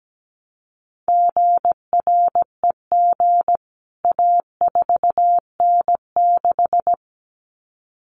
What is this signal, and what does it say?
Morse code, heard as a keyed tone, about 17 wpm, 700 Hz. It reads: GREG A4N6
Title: CWT Exchange 17wpm